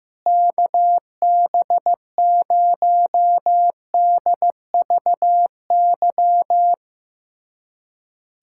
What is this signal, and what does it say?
Morse code, heard as a keyed tone, about 15 wpm, 700 Hz. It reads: KB0DVY
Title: USA Callsign DE Callsign - 6 Characters - 15wpm